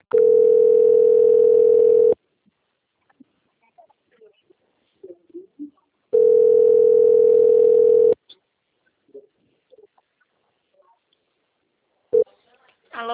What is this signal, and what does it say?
Ringback tone (North America), 440 + 480 Hz, with speech.
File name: out- - - - - .2721